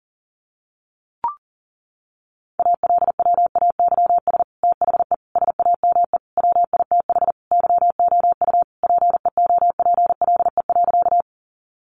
Morse code, heard as a keyed tone, about 40 wpm, 700 Hz. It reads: ALWAYS THE SAME WITH YOU PEOPLE.
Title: Sentences from Top 300 Words 40wpm